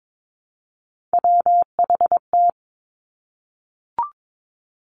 Morse code, heard as a keyed tone, about 22 wpm, 700 Hz. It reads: WHT
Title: Three Challenging Letter-Number Combinations 22wpm